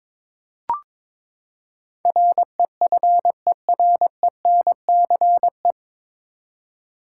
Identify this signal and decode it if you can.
Morse code, heard as a keyed tone, about 22 wpm, 700 Hz. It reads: REFERENCE
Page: Morse Code Ninja practice 22wpm